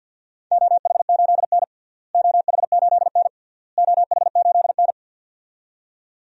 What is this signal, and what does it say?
Morse code, heard as a keyed tone, about 52 wpm, 700 Hz. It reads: OH8N OH8N OH8N
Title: DXCC-100 Call Signs - 4 Characters - Repeated 3x - 50wpm